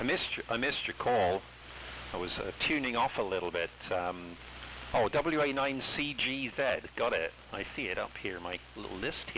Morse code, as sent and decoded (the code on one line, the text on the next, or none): none